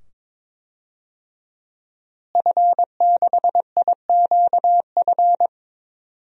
Morse code, heard as a keyed tone, about 21 wpm, 700 Hz. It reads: F6IQF
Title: DXCC-100 Call Signs - 5 Characters - Repeated 2x - 22wpm